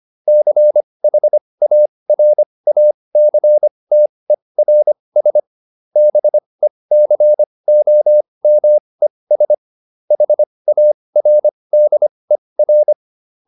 Morse code, heard as a keyed tone, about 25 wpm, spaced as for 20 wpm, 600 Hz. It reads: CHARACTERS BECOMES HARDER